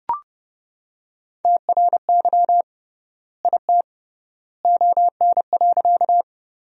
Morse code, heard as a keyed tone, about 31 wpm, 700 Hz. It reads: TRY IT ON.
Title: 3x Word-Spacing - Sentences from Top 200 Words - 30wpm